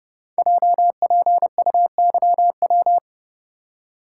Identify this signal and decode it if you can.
Morse code, heard as a keyed tone, about 28 wpm, 700 Hz. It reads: JPUYW
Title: Code Groups - 5 Letters - 30wpm